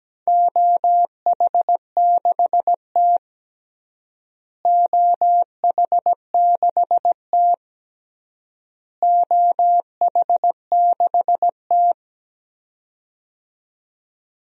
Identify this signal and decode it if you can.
Morse code, heard as a keyed tone, about 17 wpm, 700 Hz. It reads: OH6T OH6T OH6T
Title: DXCC-100 Call Signs - 4 Characters - Repeated 3x - 17wpm